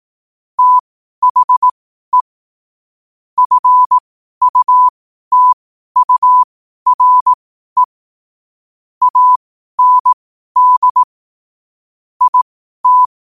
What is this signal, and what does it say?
Morse code, heard as a keyed tone, about 18 wpm, spaced as for 12 wpm, 1000 Hz. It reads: THE FUTURE AND IT
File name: qotdfiles / 12wpm